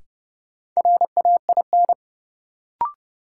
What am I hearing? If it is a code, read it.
Morse code, heard as a keyed tone, about 30 wpm, 700 Hz. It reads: RAIN
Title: Sentences from Top 600 Words - New Word Drill - 30wpm